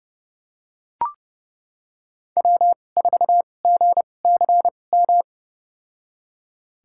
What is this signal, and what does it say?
Morse code, heard as a keyed tone, about 30 wpm, spaced as for 22 wpm, 700 Hz. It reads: W4GCM
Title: USA Call Signs - 5 Characters - 2x Character Spacing - 30wpm